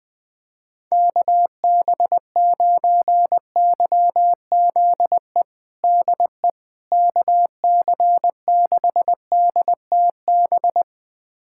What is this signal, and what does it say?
Morse code, heard as a keyed tone, about 20 wpm, 700 Hz. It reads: KB9YZE DE KC6DTB
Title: USA Callsign DE Callsign - 6 Characters - 20wpm